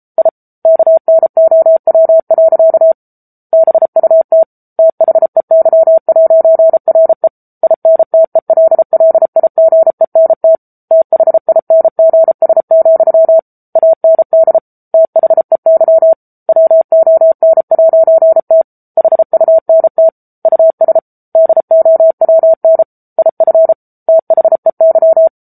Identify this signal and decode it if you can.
Morse code, heard as a keyed tone, about 34 wpm, 650 Hz. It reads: I KNOW. BUT THEY'RE INTELLIGENT THINGS, AND THEY WON'T HUNT US DOWN IF THEY
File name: Book Two / 07-Chapter 7 - 33 WPM